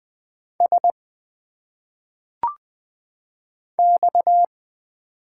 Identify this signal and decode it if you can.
Morse code, heard as a keyed tone, about 20 wpm, 700 Hz. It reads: S X
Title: Single Letters - Phonetic Pronunciation 20wpm